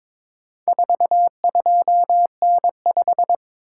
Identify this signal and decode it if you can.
Morse code, heard as a keyed tone, about 22 wpm, 700 Hz. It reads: 42N5
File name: practice / 22wpm / Four Letter-Number Combinations - 22wpm